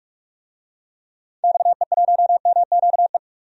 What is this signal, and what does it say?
Morse code, heard as a keyed tone, about 46 wpm, 700 Hz. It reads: XE1MQE
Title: DXCC-100 Call Signs - 6 Characters - Repeated 3x - 45wpm